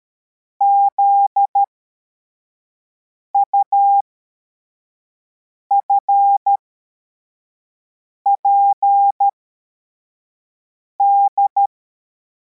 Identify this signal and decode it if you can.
Morse code, heard as a keyed tone, about 13 wpm, spaced as for 4 wpm, 800 Hz. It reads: ZUFPD
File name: formation / cw / Text74